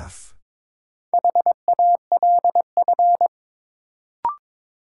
Morse code, heard as a keyed tone, about 21 wpm, 700 Hz. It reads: HALF
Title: Sets of 3 Words - New Word Drill - 22wpm